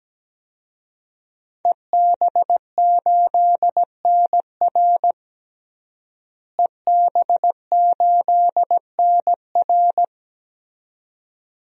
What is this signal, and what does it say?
Morse code, heard as a keyed tone, about 17 wpm, 700 Hz. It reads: EB8NR EB8NR
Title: DXCC-100 Call Signs - 5 Characters - Repeated 2x - 17wpm